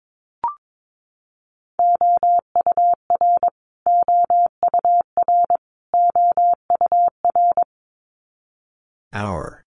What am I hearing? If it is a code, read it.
Morse code, heard as a keyed tone, about 22 wpm, 700 Hz. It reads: OUR OUR OUR
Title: Top 300 Words Repeated 22wpm